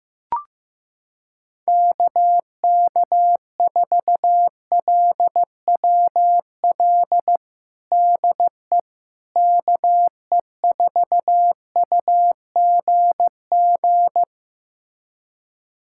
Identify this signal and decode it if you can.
Morse code, heard as a keyed tone, about 15 wpm, 700 Hz. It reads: KK4LWL DE KE4UGG
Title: USA Callsign DE Callsign - 6 Characters - 15wpm